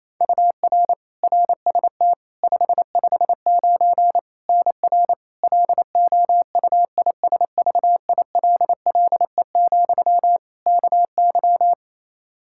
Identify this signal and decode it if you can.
Morse code, heard as a keyed tone, about 28 wpm, 700 Hz. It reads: UR RST 559 NR LOUISVILLE, KY
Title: K1USN SST QSO Practice 28wpm